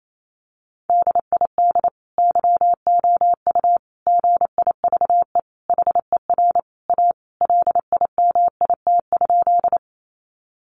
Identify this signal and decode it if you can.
Morse code, heard as a keyed tone, about 28 wpm, 700 Hz. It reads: DID YOU GIVE HER A LIMIT?
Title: Sentences from Top 1500 Words 28wpm